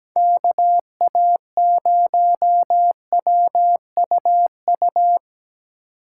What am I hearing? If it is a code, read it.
Morse code, heard as a keyed tone, about 19 wpm, 700 Hz. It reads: KA0WUU
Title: Speed Racing - USA Call Signs 17wpm